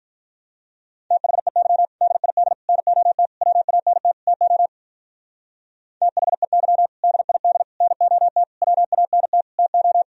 Morse code, heard as a keyed tone, about 46 wpm, 700 Hz. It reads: THEY DID NOT WANT TO THEY DID NOT WANT TO